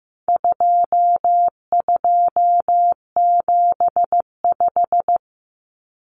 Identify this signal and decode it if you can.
Morse code, heard as a keyed tone, about 15 wpm, 700 Hz. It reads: 2275
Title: Common Cut Numbers - 4-Digit 15wpm